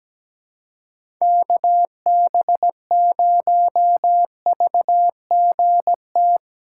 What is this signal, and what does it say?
Morse code, heard as a keyed tone, about 17 wpm, 700 Hz. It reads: KB0VGT
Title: USA Call Signs - 3x Character Spacing - 17wpm